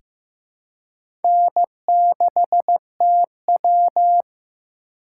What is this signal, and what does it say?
Morse code, heard as a keyed tone, about 15 wpm, 700 Hz. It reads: N6TW
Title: K1USN SST QSO Practice - 15wpm